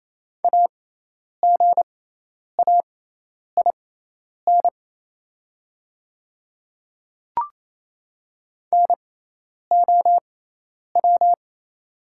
Morse code, heard as a keyed tone, about 27 wpm, spaced as for 10 wpm, 700 Hz. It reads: AGAIN NOW
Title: 6x Character Spacing - Top 200 Words - 28wpm